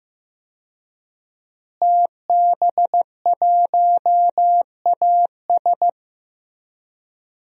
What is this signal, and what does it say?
Morse code, heard as a keyed tone, about 15 wpm, 700 Hz. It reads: TB1AS
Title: DXCC-100 Call Signs - 5 Characters - Repeated 2x - 15wpm